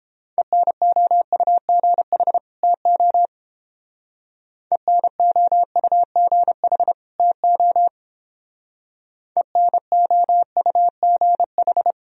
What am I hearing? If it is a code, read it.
Morse code, heard as a keyed tone, about 29 wpm, 700 Hz. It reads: ENOUGH TO ENOUGH TO ENOUGH